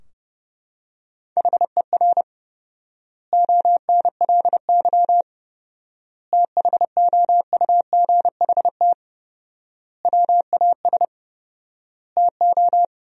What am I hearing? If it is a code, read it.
Morse code, heard as a keyed tone, about 30 wpm, 700 Hz. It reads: HER ONLY THOUGHT WAS TO
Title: 4x Word-Spacing - Sentences from Top 400 Words - 30wpm